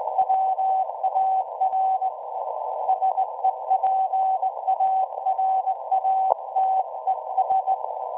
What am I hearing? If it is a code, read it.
Morse code, heard as a keyed tone, about 18 wpm, 760 Hz. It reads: WAR SEPARATES
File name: WebSDR records / SAQ at PI4THT part od 1005z 17-2kHz